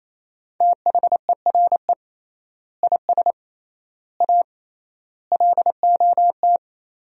Morse code, heard as a keyed tone, about 28 wpm, 700 Hz. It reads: THERE IS A LOT
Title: 3x Word-Spacing - Sentences from Top 500 Words - 28wpm